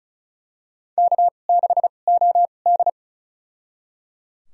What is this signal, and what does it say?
Morse code, heard as a keyed tone, about 36 wpm, spaced as for 25 wpm, 700 Hz. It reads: K6OD
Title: USA Call Signs - 4 Characters - 2x Character Spacing - 35wpm